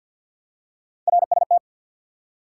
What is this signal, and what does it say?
Morse code, heard as a keyed tone, about 47 wpm, 700 Hz. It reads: RST
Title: Top 90 QSO Elements 50wpm